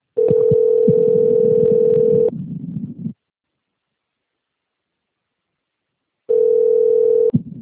Ringback tone (North America), 440 + 480 Hz.